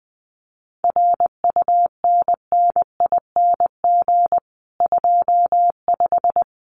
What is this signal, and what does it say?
Morse code, heard as a keyed tone, about 19 wpm, 700 Hz. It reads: RUNNING 25
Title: Common QSO Phrases 20wpm